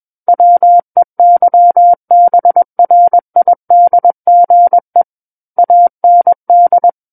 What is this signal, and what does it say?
Morse code, heard as a keyed tone, about 21 wpm, 700 Hz. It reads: WEYBRIDGE AND
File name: Book One / 12-Chapter 12 - 21 WPM